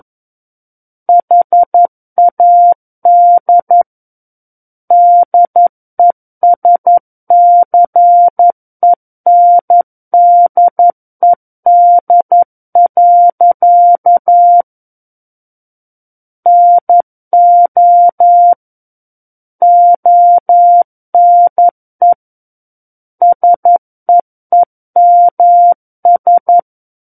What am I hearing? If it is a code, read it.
Morse code, heard as a keyed tone, about 11 wpm, 700 Hz. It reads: HAD DESCENDED. NO ONE SEEMS
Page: SKCC - War of the Worlds - 02-Chapter 2 - 11 WPM